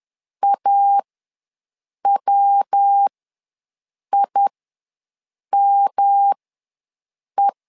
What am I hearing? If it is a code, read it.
Morse code, heard as a keyed tone, about 10 wpm, spaced as for 6 wpm, 800 Hz. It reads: AWIME
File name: LEZIONI / 3 MWRO / 3 MWRO ws 10wpm 2310